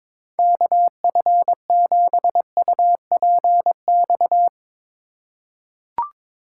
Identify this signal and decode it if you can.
Morse code, heard as a keyed tone, about 23 wpm, 700 Hz. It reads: KF7UPX
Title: USA Call Signs - 6 Characters - 4x Character Spacing - 22wpm